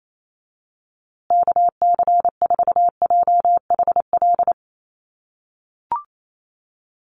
Morse code, heard as a keyed tone, about 28 wpm, 700 Hz. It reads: KC4JHL
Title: USA Call Signs Repeated 2x 28wpm